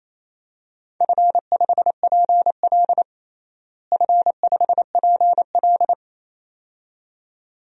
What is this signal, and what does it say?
Morse code, heard as a keyed tone, about 28 wpm, 700 Hz. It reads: F5PL F5PL
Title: DXCC-100 Call Signs - 4 Characters - Repeated 2x - 28wpm